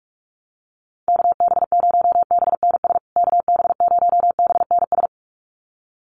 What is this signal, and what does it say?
Morse code, heard as a keyed tone, about 44 wpm, 700 Hz. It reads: KB0BNS KB0BNS
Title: USA Call Signs Repeated 2x 45wpm